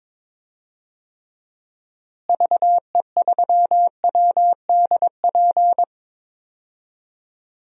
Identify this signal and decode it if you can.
Morse code, heard as a keyed tone, about 22 wpm, 700 Hz. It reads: VE3WDP
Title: DXCC-100 Call Signs - 6 Characters - Repeated 3x - 22wpm